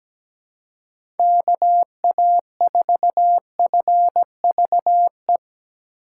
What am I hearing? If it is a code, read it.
Morse code, heard as a keyed tone, about 17 wpm, 700 Hz. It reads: KA4FVE
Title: USA Call Signs - 6 Characters - 4x Character Spacing - 17wpm